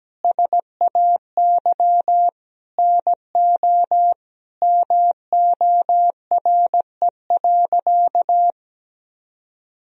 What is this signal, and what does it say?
Morse code, heard as a keyed tone, about 17 wpm, 700 Hz. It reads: SAY NO MORE.
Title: Sentences from Top 1000 Words - Review Part 4 - 17wpm